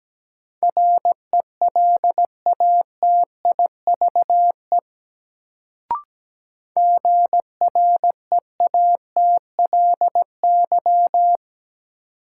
Morse code, heard as a keyed tone, about 17 wpm, 700 Hz. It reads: RELATIVE GREATLY
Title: Sentences from Top 1400 Words - New Word Drill Alt - 17wpm